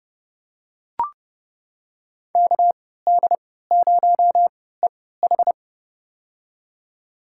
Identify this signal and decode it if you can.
Morse code, heard as a keyed tone, about 32 wpm, spaced as for 17 wpm, 700 Hz. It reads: KD0EH
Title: USA Call Signs - 3x Character Spacing - 30wpm